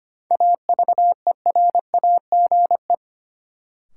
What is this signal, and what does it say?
Morse code, heard as a keyed tone, about 25 wpm, 700 Hz. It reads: AVERAGE
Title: Sentences from Top 1400 Words - New Word Drill Alt - 25wpm